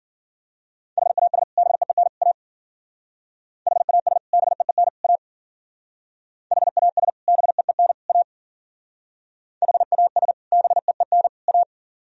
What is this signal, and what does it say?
Morse code, heard as a keyed tone, about 51 wpm, 700 Hz. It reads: HAS BEEN A HAS BEEN A HAS BEEN A HAS BEEN A